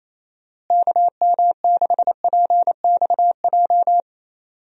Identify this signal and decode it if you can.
Morse code, heard as a keyed tone, about 28 wpm, 700 Hz. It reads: KM6PXJ
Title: USA Call Signs - 6 Characters - Repeated 3x - 28wpm